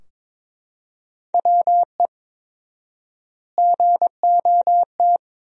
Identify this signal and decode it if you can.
Morse code, heard as a keyed tone, about 21 wpm, 700 Hz. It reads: WE GOT
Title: 4x Word-Spacing - Sentences from Top 400 Words - 22wpm